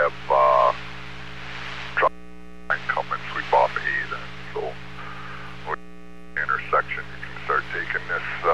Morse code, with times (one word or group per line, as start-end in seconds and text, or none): none